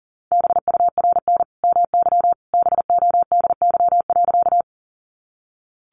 Morse code, heard as a keyed tone, about 40 wpm, 700 Hz. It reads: BURN MY BODY.
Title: Sentences from Top 1300 Words 40wpm